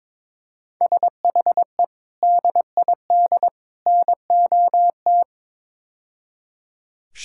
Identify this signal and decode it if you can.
Morse code, heard as a keyed tone, about 22 wpm, 700 Hz. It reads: SHE DID NOT
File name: practice / 22wpm / Build A Phrase - 5-Word Phrases - 22wpm